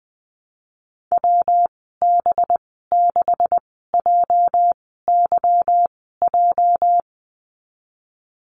Morse code, heard as a keyed tone, about 20 wpm, spaced as for 15 wpm, 700 Hz. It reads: WB6JYJ
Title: 219 Lesson 35g - Number 0 - Call Signs 20wpm